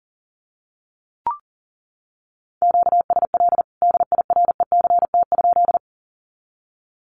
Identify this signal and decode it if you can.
Morse code, heard as a keyed tone, about 40 wpm, 700 Hz. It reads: QSL DIRECT?
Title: Common QSO Phrases 40wpm